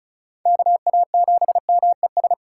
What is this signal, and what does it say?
Morse code, heard as a keyed tone, about 36 wpm, 700 Hz. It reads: KA7MES
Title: USA Call Signs - 6 Characters - 2x Character Spacing - 35wpm